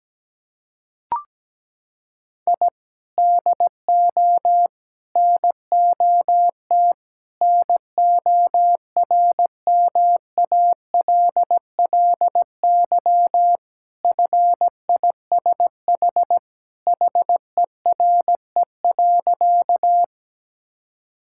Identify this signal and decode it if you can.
Morse code, heard as a keyed tone, about 17 wpm, 700 Hz. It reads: I DO NOT NORMALLY FISH HERE.
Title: Sentences from Top 1000 Words - Review Part 6 - 17wpm